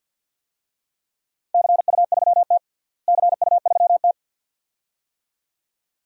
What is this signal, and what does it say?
Morse code, heard as a keyed tone, about 51 wpm, 700 Hz. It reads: CU3T CU3T
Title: DXCC-100 Call Signs - 4 Characters - Repeated 2x - 50wpm